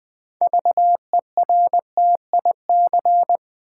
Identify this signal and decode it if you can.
Morse code, heard as a keyed tone, about 20 wpm, 700 Hz. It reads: VERTIC
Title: Build A Word - 2 Letters At A Time - 8-Letter Words - 20wpm